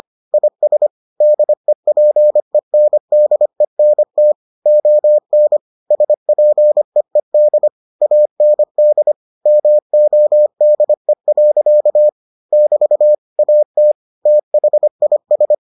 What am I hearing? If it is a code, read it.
Morse code, heard as a keyed tone, about 25 wpm, 600 Hz. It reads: IS DEPENDENT ON SPEED AND MODE. <BT> AT THIS